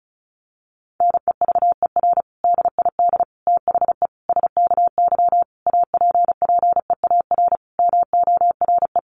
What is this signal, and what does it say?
Morse code, heard as a keyed tone, about 34 wpm, 700 Hz. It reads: NEVER DID THE SKY APPEAR MORE
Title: Sentences from Top 1200 Words 35wpm